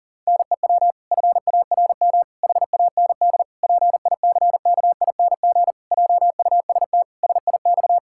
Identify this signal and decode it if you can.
Morse code, heard as a keyed tone, about 40 wpm, 700 Hz. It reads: NEW FARM HAND PICKING JUST SIX
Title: Pangram Challenge v2 - 40wpm